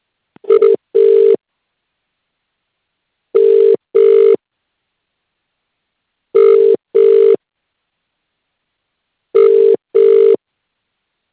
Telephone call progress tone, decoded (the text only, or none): ringback tone